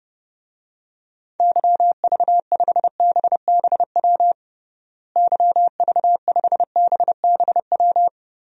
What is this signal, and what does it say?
Morse code, heard as a keyed tone, about 32 wpm, 700 Hz. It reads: YV5BBW YV5BBW
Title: DXCC-100 Call Signs - 6 Characters - Repeated 3x - 30wpm